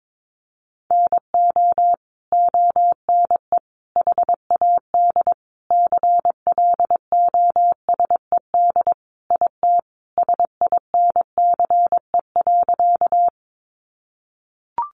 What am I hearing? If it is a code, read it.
Morse code, heard as a keyed tone, about 22 wpm, 700 Hz. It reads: NO ONE HAD CLOSED IT SINCE.
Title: Sentences from Top 1400 Words 22wpm